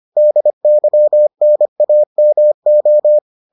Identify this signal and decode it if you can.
Morse code, heard as a keyed tone, about 25 wpm, 600 Hz. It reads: DYNAMO